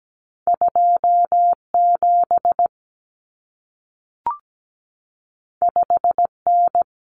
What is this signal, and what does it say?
Morse code, heard as a keyed tone, about 17 wpm, 700 Hz. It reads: 27 5N
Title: Common Cut Numbers - 2-Digit 17wpm